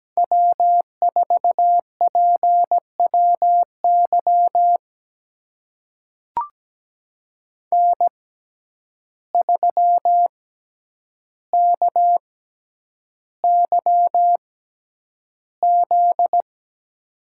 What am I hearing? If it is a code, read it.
Morse code, heard as a keyed tone, about 17 wpm, spaced as for 9 wpm, 700 Hz. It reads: W4PWY N3KYZ
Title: USA Call Signs - 5 Characters - 6x Character Spacing - 17wpm